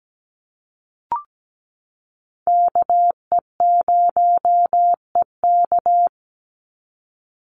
Morse code, heard as a keyed tone, about 17 wpm, 700 Hz. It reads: KE0EK
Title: Call Signs - 17wpm